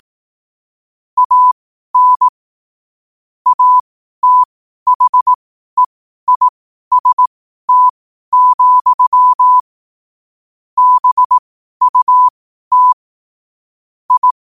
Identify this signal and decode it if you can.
Morse code, heard as a keyed tone, about 18 wpm, spaced as for 12 wpm, 1000 Hz. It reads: AN ATHEIST, BUT I
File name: qotdfiles / 12wpm